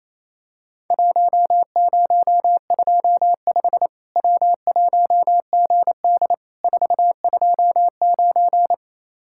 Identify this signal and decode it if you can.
Morse code, heard as a keyed tone, about 28 wpm, 700 Hz. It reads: 1025 W1GD 429
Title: Worked All Europe DX Contest - QTC Exchange - 28wpm